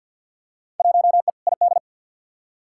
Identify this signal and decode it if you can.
Morse code, heard as a keyed tone, about 52 wpm, 700 Hz. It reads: 1E ID
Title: Field Day Exchanges - 50wpm